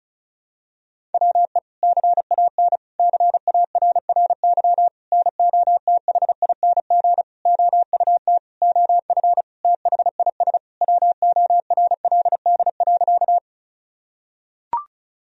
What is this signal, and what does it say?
Morse code, heard as a keyed tone, about 36 wpm, 700 Hz. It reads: WE CAN CARRY NOTHING OUT OF THIS WORLD.
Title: Sentences from Top 1000 Words - Review Part 1 - 35wpm